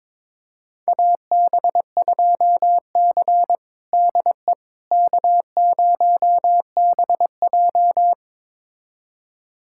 Morse code, heard as a keyed tone, about 21 wpm, 700 Hz. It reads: AB2C DE K0BJ
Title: USA Callsign DE Callsign - 4 Characters - 22wpm